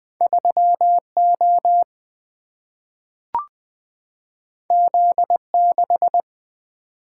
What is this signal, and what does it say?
Morse code, heard as a keyed tone, about 20 wpm, 700 Hz. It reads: 3 O Z 6
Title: Single-Letter Single-Number 20wpm